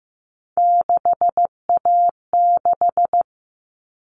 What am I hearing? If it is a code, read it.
Morse code, heard as a keyed tone, about 15 wpm, 700 Hz. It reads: 6A6
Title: All Cut Numbers - 3-Digit 15wpm